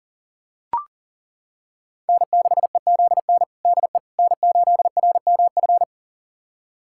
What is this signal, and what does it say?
Morse code, heard as a keyed tone, about 37 wpm, 700 Hz. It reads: N6EZN DE N8RMF
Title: USA Callsign DE Callsign - 5 Characters - 40wpm